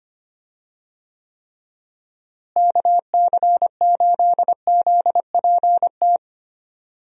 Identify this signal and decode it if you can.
Morse code, heard as a keyed tone, about 25 wpm, 700 Hz. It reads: KC8ZPT
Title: USA Call Signs - 6 Characters - Repeated 3x - 25wpm